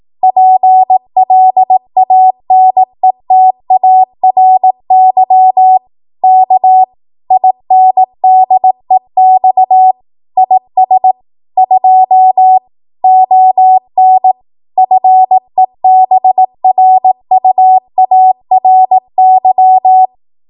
Morse code, heard as a keyed tone, about 18 wpm, 750 Hz. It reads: PLANETARY K INDEX IS 2 ON FEBRUARY